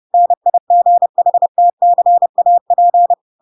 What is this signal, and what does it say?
Morse code, heard as a keyed tone, about 30 wpm, 700 Hz. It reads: NIGHTCAP
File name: Pinocchio / Pinocchio ch7 30 30 ik1hge